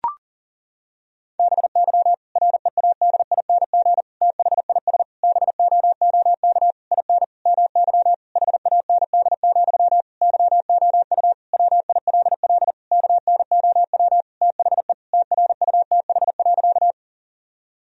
Morse code, heard as a keyed tone, about 40 wpm, 700 Hz. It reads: BY READING THIS BOOK IN MY HAND, YOU WILL KNOW THE TRUTH.
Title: Sentences from Top 1000 Words - Review Part 3 - 40wpm